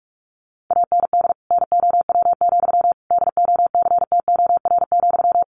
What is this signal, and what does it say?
Morse code, heard as a keyed tone, about 44 wpm, 700 Hz. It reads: AND NOW, DOCTOR,
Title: Sentences from Top 1500 Words 45wpm